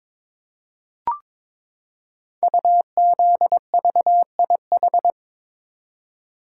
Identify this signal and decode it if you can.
Morse code, heard as a keyed tone, about 22 wpm, 700 Hz. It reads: UZVIH
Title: Code Groups - 5 Letters - 22wpm